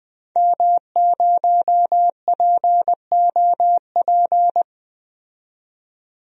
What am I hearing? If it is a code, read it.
Morse code, heard as a keyed tone, about 20 wpm, 700 Hz. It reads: M0POP
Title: DXCC-100 Call Signs - 5 Characters - 20wpm